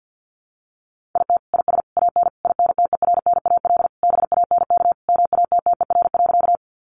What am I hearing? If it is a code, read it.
Morse code, heard as a keyed tone, about 50 wpm, 700 Hz. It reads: IT IS AN INTERNAL BANK MATTER.
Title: Sentences from Top 1400 Words 50wpm